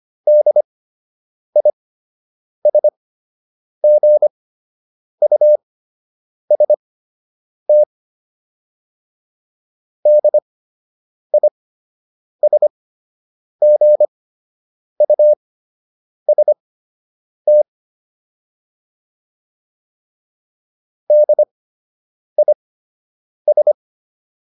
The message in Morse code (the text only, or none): DISGUST DISGUST DIS